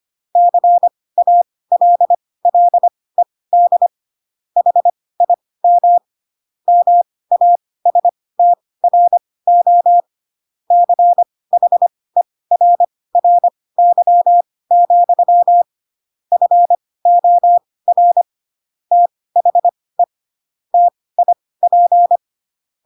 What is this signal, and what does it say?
Morse code, heard as a keyed tone, about 25 wpm, spaced as for 18 wpm, 700 Hz. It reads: CALLED HIM MASTRO CHERRY, FOR THE TIP
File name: Pinocchio / Pinocchio ch1 12 25 ik1hge